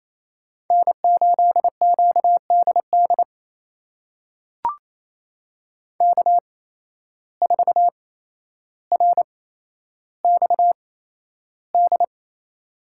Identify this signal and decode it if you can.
Morse code, heard as a keyed tone, about 28 wpm, spaced as for 12 wpm, 700 Hz. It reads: N8QDD K4RXD
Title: USA Call Signs - 5 Characters - 8x Character Spacing - 28wpm